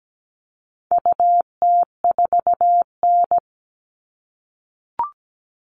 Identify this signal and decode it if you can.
Morse code, heard as a keyed tone, about 17 wpm, 700 Hz. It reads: UT4N
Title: All Cut Numbers - 4-Digit 17wpm